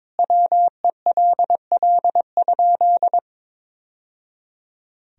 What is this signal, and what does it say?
Morse code, heard as a keyed tone, about 22 wpm, 700 Hz. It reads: WELL?
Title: Sentences from Top 1000 Words - Review Part 4 - 22wpm